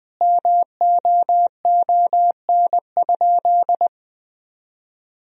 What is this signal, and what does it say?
Morse code, heard as a keyed tone, about 20 wpm, 700 Hz. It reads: MOON?
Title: Sentences from Top 1000 Words - Review Part 3 - 20wpm